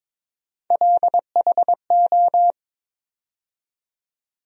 Morse code, heard as a keyed tone, about 22 wpm, 700 Hz. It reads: LHO
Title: USA Call Sign Postfixes - 3 Letters 22wpm